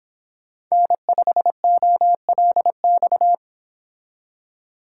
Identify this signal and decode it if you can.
Morse code, heard as a keyed tone, about 26 wpm, 700 Hz. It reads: N5OLX